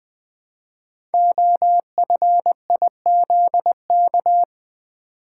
Morse code, heard as a keyed tone, about 20 wpm, 700 Hz. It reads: OFIZK
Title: Code Groups - 5 Letters - 20wpm